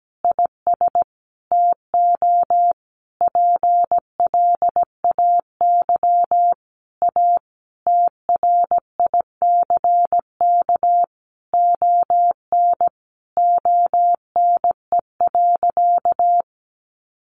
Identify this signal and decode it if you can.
Morse code, heard as a keyed tone, about 17 wpm, 700 Hz. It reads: IS TO PLAY A TRICK ON ONE.
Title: Sentences from Top 1500 Words 17wpm